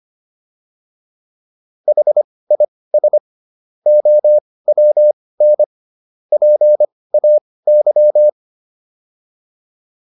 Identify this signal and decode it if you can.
Morse code, heard as a keyed tone, about 25 wpm, spaced as for 18 wpm, 600 Hz. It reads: HIS OWN PAY